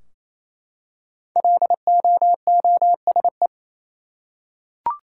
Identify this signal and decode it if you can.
Morse code, heard as a keyed tone, about 28 wpm, 700 Hz. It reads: LOOSE
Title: Five Letter Words - 28wpm